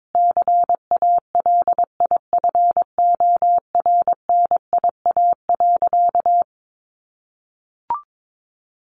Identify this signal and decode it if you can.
Morse code, heard as a keyed tone, about 22 wpm, 700 Hz. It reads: CALIFORNIA.
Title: Easy Phrases 22wpm